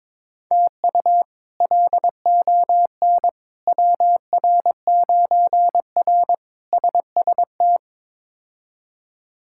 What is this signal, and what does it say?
Morse code, heard as a keyed tone, about 22 wpm, 700 Hz. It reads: TU LON WR9R SST
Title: K1USN SST QSO Practice - 22wpm